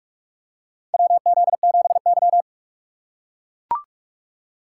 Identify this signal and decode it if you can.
Morse code, heard as a keyed tone, about 44 wpm, 700 Hz. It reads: WZ7Y
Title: USA Call Signs - 4 Characters - 6x Character Spacing - 45wpm